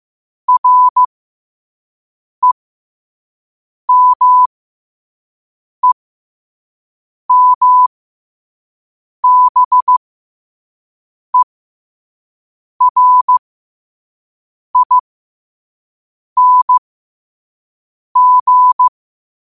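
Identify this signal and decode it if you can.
Morse code, heard as a keyed tone, about 15 wpm, spaced as for 5 wpm, 1000 Hz. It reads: REMEMBERING
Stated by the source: Morse code sent at 5 wpm for learning.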